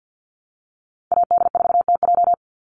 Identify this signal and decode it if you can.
Morse code, heard as a keyed tone, about 50 wpm, 700 Hz. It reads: AD4TJ
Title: Call Signs Repeated 3x - 50wpm